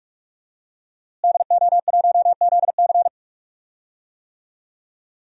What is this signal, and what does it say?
Morse code, heard as a keyed tone, about 46 wpm, 700 Hz. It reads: DO1ZC
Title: DXCC-100 Call Signs - 5 Characters - Repeated 3x - 45wpm